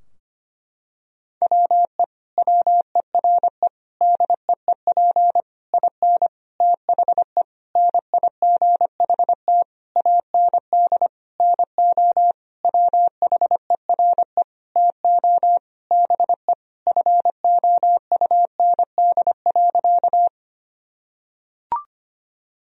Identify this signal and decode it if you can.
Morse code, heard as a keyed tone, about 25 wpm, 700 Hz. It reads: WE WERE DEEP IN THE NIGHT AND NO WHERE TO BE FOUND.
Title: Sentences from Top 1000 Words - Review Part 3 - 25wpm